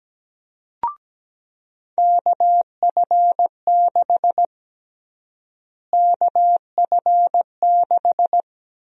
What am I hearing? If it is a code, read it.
Morse code, heard as a keyed tone, about 17 wpm, 700 Hz. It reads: KF6 KF6
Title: USA Call Sign Prefixes - Repeated 2x - 17wpm